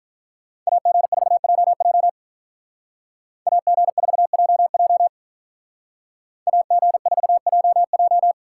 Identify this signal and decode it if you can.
Morse code, heard as a keyed tone, about 44 wpm, 700 Hz. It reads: AG4JJ AG4JJ AG4JJ